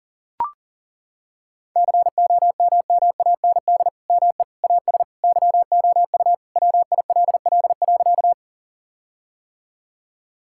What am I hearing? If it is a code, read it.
Morse code, heard as a keyed tone, about 37 wpm, 700 Hz. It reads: COMMAND ME AS YOU WILL.
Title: Sentences from Top 1600 Words 40wpm